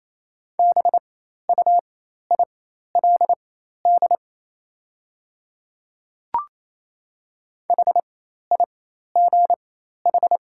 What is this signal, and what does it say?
Morse code, heard as a keyed tone, about 27 wpm, spaced as for 13 wpm, 700 Hz. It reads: BUILD HIGH
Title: 4x Character Spacing - Top 200 Words - 28wpm